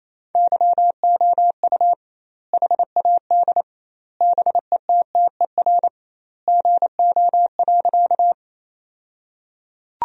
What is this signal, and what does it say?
Morse code, heard as a keyed tone, about 28 wpm, 700 Hz. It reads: YOU HAD BETTER GO.
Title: 2x Word-Spacing - Sentences from Top 400 Words - 28wpm